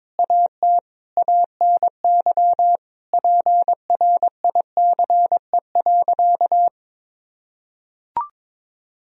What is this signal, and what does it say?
Morse code, heard as a keyed tone, about 22 wpm, 700 Hz. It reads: AT ANY PRICE.
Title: Sentences from Top 1000 Words - Review Part 4 - 22wpm